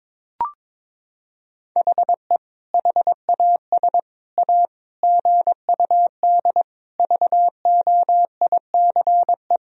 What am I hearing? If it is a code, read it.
Morse code, heard as a keyed tone, about 21 wpm, 700 Hz. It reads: HE HAS A GUD VOICE
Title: Sentences from Top 500 Words and Abbreviations 22wpm